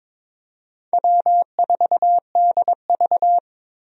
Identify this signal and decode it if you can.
Morse code, heard as a keyed tone, about 21 wpm, 700 Hz. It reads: W4DV
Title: USA Call Signs - 4 Characters - Repeated 2x - 22wpm